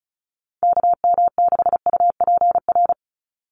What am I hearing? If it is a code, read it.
Morse code, heard as a keyed tone, about 34 wpm, 700 Hz. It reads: KM6UPR
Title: USA Call Signs 35wpm